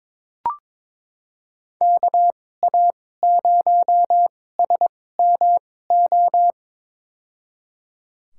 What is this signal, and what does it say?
Morse code, heard as a keyed tone, about 22 wpm, spaced as for 16 wpm, 700 Hz. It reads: KA0SMO
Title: USA Call Signs - 6 Characters - 2x Character Spacing - 22wpm